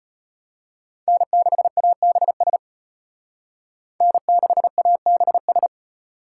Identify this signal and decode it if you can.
Morse code, heard as a keyed tone, about 31 wpm, 700 Hz. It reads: N6ABS N6ABS
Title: Speed Racing - USA 1x3 Call Signs 25wpm